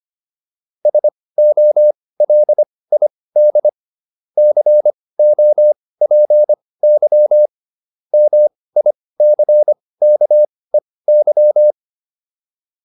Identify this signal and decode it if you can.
Morse code, heard as a keyed tone, about 25 wpm, spaced as for 18 wpm, 600 Hz. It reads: SOLID COPY MICKEY